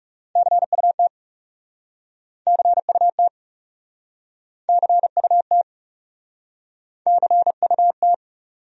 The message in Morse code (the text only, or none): CUT CUT CUT CUT